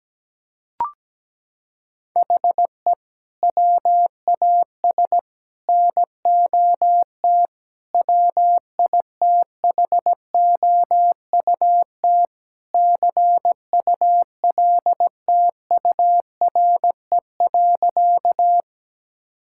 Morse code, heard as a keyed tone, about 17 wpm, 700 Hz. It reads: HE WAS NOT WITHOUT CULTURE.
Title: Sentences from Top 1000 Words - Review Part 6 - 17wpm